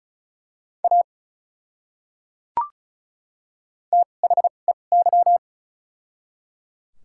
Morse code, heard as a keyed tone, about 36 wpm, spaced as for 25 wpm, 700 Hz. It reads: A THEY
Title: 2x Character Spacing - Top 100 Words - 35wpm